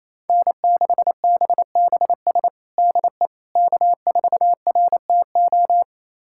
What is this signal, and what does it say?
Morse code, heard as a keyed tone, about 28 wpm, 700 Hz. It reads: N6BBS DE K4RTO
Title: USA Callsign DE Callsign - 5 Characters - 28wpm